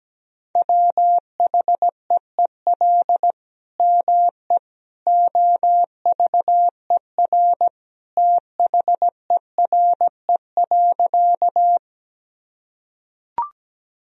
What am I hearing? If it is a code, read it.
Morse code, heard as a keyed tone, about 17 wpm, 700 Hz. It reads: WHEEL ME OVER THERE.
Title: Sentences from Top 1000 Words - Review Part 3 - 17wpm